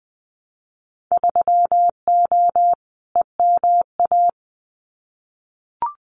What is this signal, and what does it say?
Morse code, heard as a keyed tone, about 20 wpm, 700 Hz. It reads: 3O EMA
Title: Winter Field Day Exchanges 20wpm